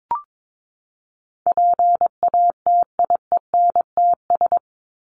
Morse code, heard as a keyed tone, about 22 wpm, 700 Hz. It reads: PATIENTS
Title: ES:S Suffix - Top 100 Words with Base Words 22wpm